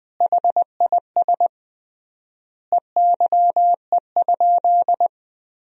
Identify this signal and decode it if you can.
Morse code, heard as a keyed tone, about 20 wpm, 700 Hz. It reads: HIS EYE?
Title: 3x Word-Spacing - Sentences from Top 300 Words - 20wpm